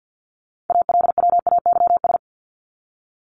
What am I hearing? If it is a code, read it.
Morse code, heard as a keyed tone, about 50 wpm, 700 Hz. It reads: ALWAYS
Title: Top 300 Words 50wpm